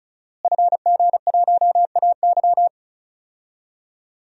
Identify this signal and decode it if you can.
Morse code, heard as a keyed tone, about 36 wpm, 700 Hz. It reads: FG1AY
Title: DXCC-100 Call Signs - 5 Characters - Repeated 3x - 35wpm